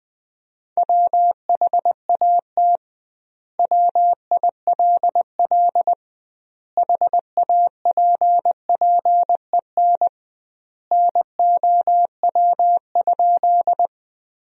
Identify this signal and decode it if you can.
Morse code, heard as a keyed tone, about 20 wpm, 700 Hz. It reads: WHAT WILL HAPPEN NOW?
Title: 2x Word-Spacing - Sentences from Top 400 Words - 20wpm